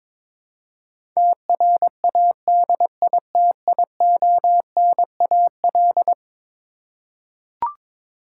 Morse code, heard as a keyed tone, about 22 wpm, 700 Hz. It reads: TRADITIONAL
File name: practice / 22wpm / Eleven Letter Words - 22wpm